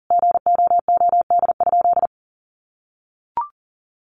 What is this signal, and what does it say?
Morse code, heard as a keyed tone, about 40 wpm, 700 Hz. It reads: GOOD?
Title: Sentences from Top 200 Words 40wpm